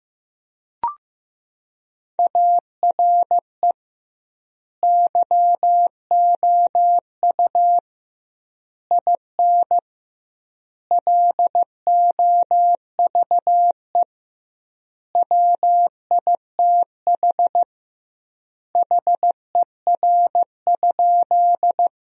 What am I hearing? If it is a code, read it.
Morse code, heard as a keyed tone, about 15 wpm, 700 Hz. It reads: ARE YOU IN LOVE WITH HER?
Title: 2x Word-Spacing - Sentences from Top 400 Words - 15wpm